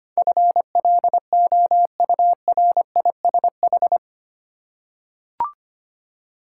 Morse code, heard as a keyed tone, about 25 wpm, 700 Hz. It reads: FLOURISH
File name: Eight Letter Words - 25wpm